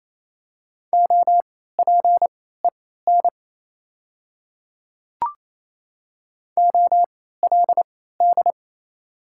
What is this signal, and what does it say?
Morse code, heard as a keyed tone, about 28 wpm, spaced as for 16 wpm, 700 Hz. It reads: OPEN OLD
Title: 3x Character Spacing - Top 500 Words - 28wpm